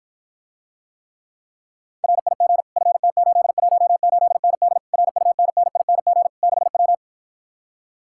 Morse code, heard as a keyed tone, about 53 wpm, 700 Hz. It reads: RIG FT818ND RUNNING 6W